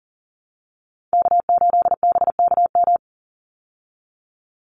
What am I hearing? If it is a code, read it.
Morse code, heard as a keyed tone, about 40 wpm, 700 Hz. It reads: K8BKM
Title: Call Signs Repeated 3x - 40wpm